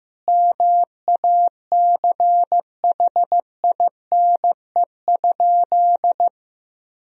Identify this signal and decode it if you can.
Morse code, heard as a keyed tone, about 16 wpm, 700 Hz. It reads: MACHINE?
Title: Sentences from Top 1000 Words - Review Part 2 - 15wpm